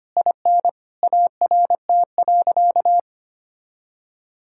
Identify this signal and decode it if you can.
Morse code, heard as a keyed tone, about 25 wpm, 700 Hz. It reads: IN ART.
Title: Sentences from Top 1000 Words - Review Part 6 - 25wpm